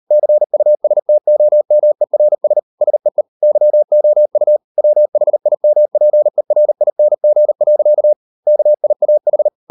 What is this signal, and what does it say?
Morse code, heard as a keyed tone, about 38 wpm, 600 Hz. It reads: CUSTOMERS SEE YOU WHIMPERING. KIAH